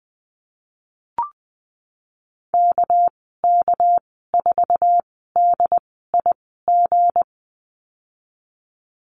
Morse code, heard as a keyed tone, about 20 wpm, spaced as for 15 wpm, 700 Hz. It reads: KK4DIG
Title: 213 Lesson 34g - Number 8 - Call Signs 20wpm